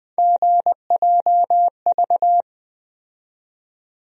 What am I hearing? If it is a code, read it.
Morse code, heard as a keyed tone, about 20 wpm, 700 Hz. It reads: GJV
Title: USA Call Sign Postfixes - 3 Letters 20wpm